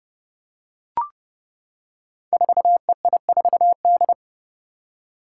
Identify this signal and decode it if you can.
Morse code, heard as a keyed tone, about 29 wpm, 700 Hz. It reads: 4EI4D
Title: Five Challenging Letter-Number Combinations 30wpm